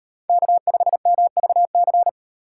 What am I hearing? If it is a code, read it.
Morse code, heard as a keyed tone, about 36 wpm, 700 Hz. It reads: K5MVC